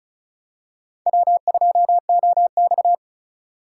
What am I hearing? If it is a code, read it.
Morse code, heard as a keyed tone, about 36 wpm, 700 Hz. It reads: W2OX
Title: USA Call Signs - 3x Character Spacing - 35wpm